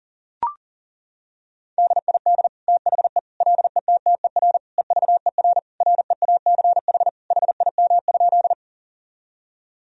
Morse code, heard as a keyed tone, about 37 wpm, 700 Hz. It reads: DID THE LETTER EVER REACH HIM?
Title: Sentences from Top 1000 Words - Review Part 2 - 40wpm